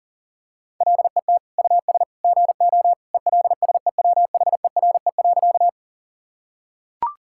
Morse code, heard as a keyed tone, about 40 wpm, 700 Hz. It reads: LET US GO ELSEWHERE.
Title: Sentences from Top 1600 Words 40wpm